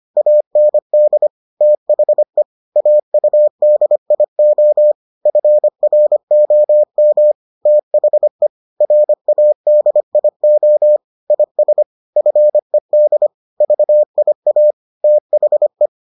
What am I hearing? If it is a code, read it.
Morse code, heard as a keyed tone, about 25 wpm, 600 Hz. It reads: AND THE AUDIO FROM THE RADIO IS FED VIA THE